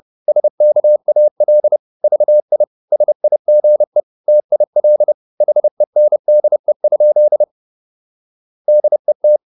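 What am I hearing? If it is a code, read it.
Morse code, heard as a keyed tone, about 30 wpm, 600 Hz. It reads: SKAL VI SIGE TIL HENDE? DET
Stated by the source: High-speed Morse (QRQ) at 30 wpm.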